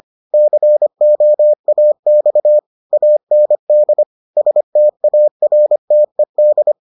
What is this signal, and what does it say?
Morse code, heard as a keyed tone, about 25 wpm, 600 Hz. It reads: COAX AND STARTED